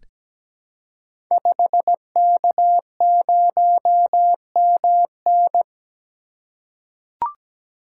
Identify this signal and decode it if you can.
Morse code, heard as a keyed tone, about 17 wpm, 690 Hz. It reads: 5K0MN
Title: Code Groups - 5 Letter:Numbers - 17wpm